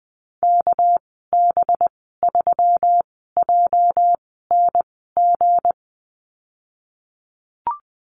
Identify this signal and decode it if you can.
Morse code, heard as a keyed tone, about 19 wpm, spaced as for 14 wpm, 700 Hz. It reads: KB3JNG
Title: 207 Lesson 33g - Letter j - Call Signs 20wpm